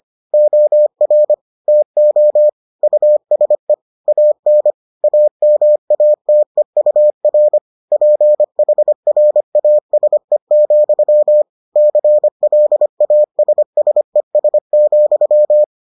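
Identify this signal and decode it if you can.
Morse code, heard as a keyed tone, about 25 wpm, 600 Hz. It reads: OR TO USE AN AMATEUR PHRASE, CLASSES,
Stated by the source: Morse edition of the Foundations of Amateur Radio episodes.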